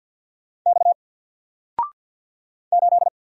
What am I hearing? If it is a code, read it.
Morse code, heard as a keyed tone, about 48 wpm, 700 Hz. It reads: X 8
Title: Single Character (Letter:Number:Punctuation:Prosign) Alt 50wpm